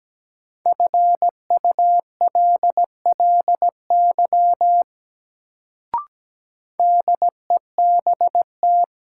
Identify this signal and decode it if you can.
Morse code, heard as a keyed tone, about 17 wpm, 700 Hz. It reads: FULLY DEBT
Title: Sentences from Top 1300 Words - New Word Drill - 17wpm